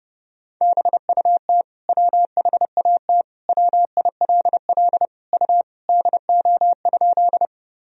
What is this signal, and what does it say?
Morse code, heard as a keyed tone, about 30 wpm, 700 Hz. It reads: BUT WHAT WILL U DO?
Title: Sentences from Top 100 Words and Abbreviations 30wpm